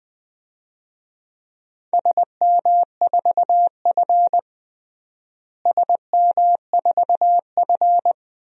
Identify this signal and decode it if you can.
Morse code, heard as a keyed tone, about 20 wpm, 700 Hz. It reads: SM4F SM4F
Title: DXCC-100 Call Signs - 4 Characters - Repeated 2x - 20wpm